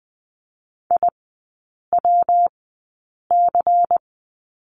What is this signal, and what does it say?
Morse code, heard as a keyed tone, about 20 wpm, spaced as for 8 wpm, 690 Hz. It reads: IWC
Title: 212 Lesson 34e - Number 8 - Three Characters 20wpm